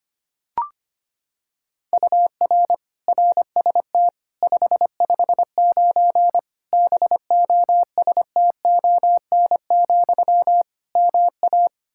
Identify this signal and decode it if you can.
Morse code, heard as a keyed tone, about 25 wpm, 700 Hz. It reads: UR RST 559 BOSTON, MA
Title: Practice QSO - Exchange 1 - Beginning - 25wpm